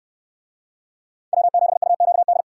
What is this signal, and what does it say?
Morse code, heard as a keyed tone, about 64 wpm, 700 Hz. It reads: W7UQD